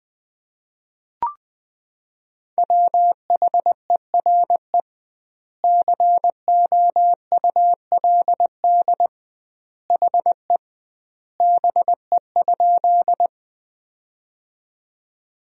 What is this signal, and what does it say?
Morse code, heard as a keyed tone, about 20 wpm, 700 Hz. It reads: WHERE COULD HE BE?
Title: 2x Word-Spacing - Sentences from Top 200 Words - 20wpm